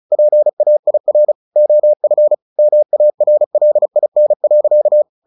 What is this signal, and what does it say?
Morse code, heard as a keyed tone, about 34 wpm, 600 Hz. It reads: PAIR OF MARLIN.
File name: the old man and the sea / the old man and the sea 1 35wpm